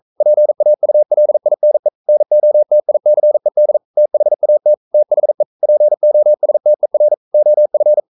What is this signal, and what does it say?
Morse code, heard as a keyed tone, about 42 wpm, 600 Hz. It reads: PAULINE NOTICED THAT THE POSTER OF
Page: Romanen Never af Ken Follet på engelsk.